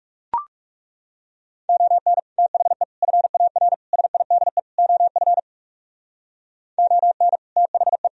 Morse code, heard as a keyed tone, about 36 wpm, 700 Hz. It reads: ON THE FAR SIDE OF ON THE
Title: Speed-Racing - Sets of 5 Words 30wpm